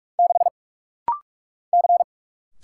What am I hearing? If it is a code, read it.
Morse code, heard as a keyed tone, about 44 wpm, 700 Hz. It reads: 6 C
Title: Single Letter-Number - Rapid-Fire 45wpm